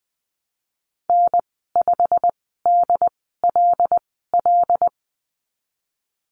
Morse code, heard as a keyed tone, about 20 wpm, spaced as for 15 wpm, 700 Hz. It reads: N5DLL
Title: 161 Lesson 26g - Letter g - Call Signs 20wpm